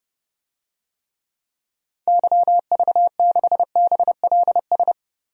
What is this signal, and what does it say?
Morse code, heard as a keyed tone, about 30 wpm, 700 Hz. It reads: YV6BLS
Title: DXCC-100 Call Signs - 6 Characters - Repeated 2x - 30wpm